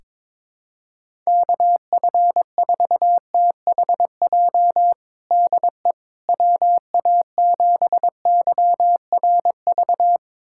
Morse code, heard as a keyed tone, about 22 wpm, 700 Hz. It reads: KF4THJ DE WA7YRV
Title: USA Callsign DE Callsign 22wpm